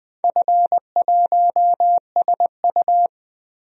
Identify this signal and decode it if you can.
Morse code, heard as a keyed tone, about 20 wpm, 700 Hz. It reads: F1SU
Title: DXCC-100 Call Signs - 4 Characters - Repeated 2x - 20wpm